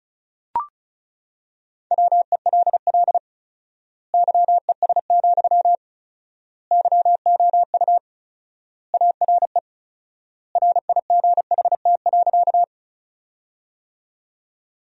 Morse code, heard as a keyed tone, about 36 wpm, 700 Hz. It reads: WELL YES, YOU ARE RIGHT.
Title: 4x Word-Spacing - Sentences from Top 500 Words - 35wpm